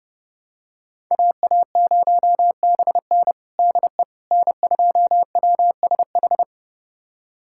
Morse code, heard as a keyed tone, about 30 wpm, 700 Hz. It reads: AA0BN DE N2WSH
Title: USA Callsign DE Callsign - 5 Characters - 30wpm